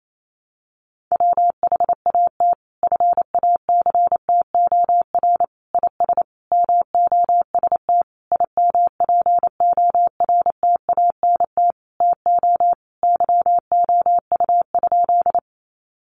Morse code, heard as a keyed tone, about 28 wpm, 700 Hz. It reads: WHAT FACTOR IS MOST IMPORTANT TO YOU?
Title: Sentences from Top 1300 Words 28wpm